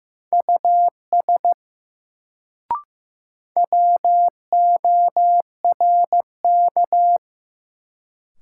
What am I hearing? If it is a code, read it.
Morse code, heard as a keyed tone, about 15 wpm, 700 Hz. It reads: US WORK
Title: Sentences from Top 200 Words - New Word Drill - 15wpm